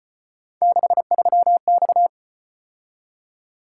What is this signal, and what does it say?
Morse code, heard as a keyed tone, about 33 wpm, 700 Hz. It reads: 63X